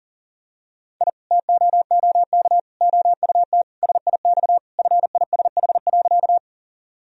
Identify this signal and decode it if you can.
Morse code, heard as a keyed tone, about 40 wpm, 700 Hz. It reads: I TOOK OUT SIX FISH.
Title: Sentences from Top 1000 Words - Review Part 2 - 40wpm